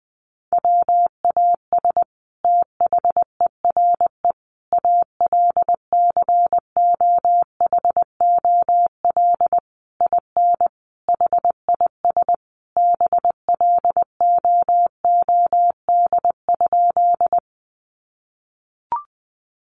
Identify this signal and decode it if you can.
Morse code, heard as a keyed tone, about 20 wpm, 700 Hz. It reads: WAS THERE ALCOHOL IN HIS BLOOD?
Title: Sentences from Top 1400 Words 20wpm